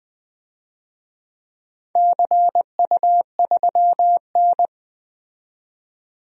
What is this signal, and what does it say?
Morse code, heard as a keyed tone, about 20 wpm, 700 Hz. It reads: CU3N
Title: DXCC-100 Call Signs - 4 Characters - 20wpm